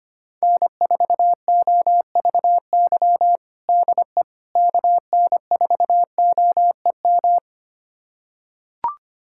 Morse code, heard as a keyed tone, about 25 wpm, 700 Hz. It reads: N4OVY DE KN4OEM
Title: USA Callsign DE Callsign 25wpm